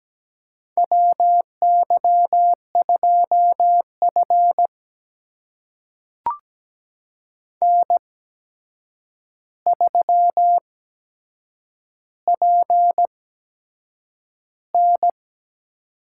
Morse code, heard as a keyed tone, about 17 wpm, spaced as for 7 wpm, 700 Hz. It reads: WY2F N3PN
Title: USA Call Signs - 4 Characters - 8x Character Spacing - 17wpm